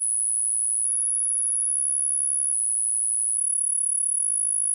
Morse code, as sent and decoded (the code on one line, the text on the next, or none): none